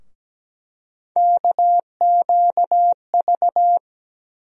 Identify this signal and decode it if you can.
Morse code, heard as a keyed tone, about 17 wpm, 700 Hz. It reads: KQV
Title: USA Call Sign Postfixes - 3 Letters 17wpm